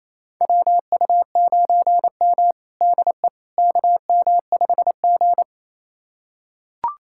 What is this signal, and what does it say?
Morse code, heard as a keyed tone, about 27 wpm, 700 Hz. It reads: WU9M DE KM5G
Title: USA Callsign DE Callsign - 4 Characters - 28wpm